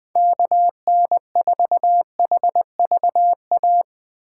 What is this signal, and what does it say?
Morse code, heard as a keyed tone, about 20 wpm, 700 Hz. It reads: KN4HVA
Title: Speed Racing - USA 2x3 Call Signs 17wpm